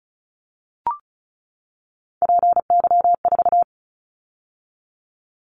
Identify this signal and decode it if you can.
Morse code, heard as a keyed tone, about 36 wpm, 700 Hz. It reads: PY4
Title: Callsign Prefixes for Top 50 DXCC - 35wpm